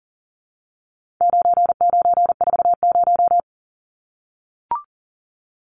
Morse code, heard as a keyed tone, about 40 wpm, 700 Hz. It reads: 9940
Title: Numbers - 4-Digit 40wpm